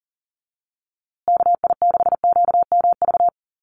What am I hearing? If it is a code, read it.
Morse code, heard as a keyed tone, about 40 wpm, 700 Hz. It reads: KI6QMV
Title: USA Call Signs Repeated 3x 40wpm